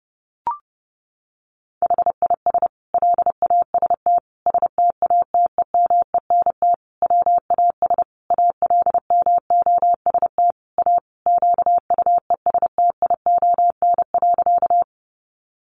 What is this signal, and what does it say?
Morse code, heard as a keyed tone, about 30 wpm, 700 Hz. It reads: HIS LAST STATEMENT WAS ALMOST A QUESTION.
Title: Sentences from Top 1200 Words 30wpm